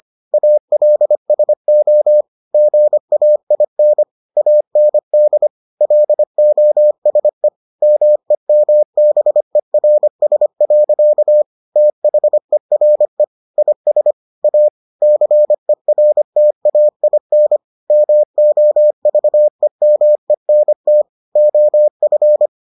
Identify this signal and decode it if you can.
Morse code, heard as a keyed tone, about 25 wpm, 600 Hz. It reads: ALSO GAIN AND LOSE MEMBERS. THERE IS A CERTAIN MOVEMENT OF